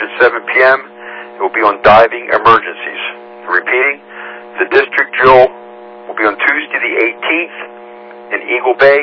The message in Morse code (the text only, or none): none